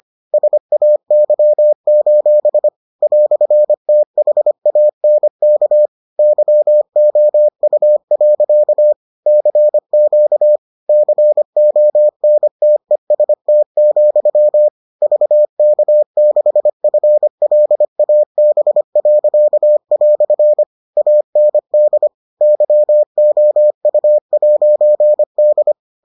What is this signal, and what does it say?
Morse code, heard as a keyed tone, about 25 wpm, 600 Hz. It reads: SAY: "THANK YOU. CQ CONTEST, VK6FLAB." AND YOU'D